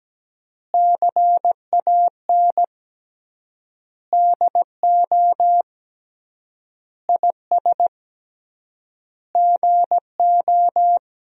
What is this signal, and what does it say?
Morse code, heard as a keyed tone, about 17 wpm, 700 Hz. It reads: CAN DO IS GO
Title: 3x Word-Spacing - Sentences from Top 200 Words - 17wpm